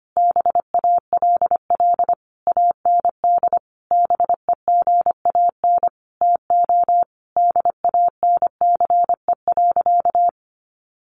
Morse code, heard as a keyed tone, about 25 wpm, 700 Hz. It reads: BALL AND BEGAN TO DANCE.
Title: Sentences from Top 900 Words 25wpm